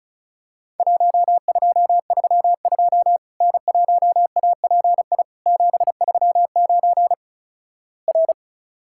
Morse code, heard as a keyed tone, about 34 wpm, 700 Hz. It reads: 1232 N1API 739 R
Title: Worked All Europe DX Contest - QTC Exchange - 35wpm